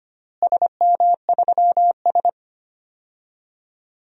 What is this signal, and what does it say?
Morse code, heard as a keyed tone, about 25 wpm, 700 Hz. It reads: SM3S
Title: DXCC-100 Call Signs - 4 Characters - Repeated 3x - 25wpm